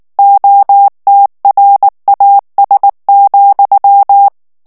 Morse code, heard as a keyed tone, about 19 wpm, 800 Hz. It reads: OTRAS,
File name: morse mp3 / ESPAGNOL TEXTE EN CLAIR 2 VITESSE 18 MOTS MINUTES